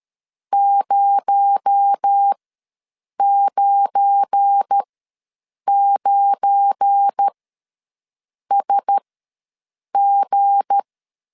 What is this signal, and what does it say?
Morse code, heard as a keyed tone, about 13 wpm, spaced as for 7 wpm, 800 Hz. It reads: 099SG
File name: LEZIONI / 7 GF9Z / 7 GF9 ws 12wpm 3344